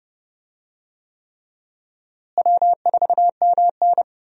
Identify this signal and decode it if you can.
Morse code, heard as a keyed tone, about 32 wpm, 700 Hz. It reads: W4MN
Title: USA Call Signs - 4 Characters - Repeated 3x - 30wpm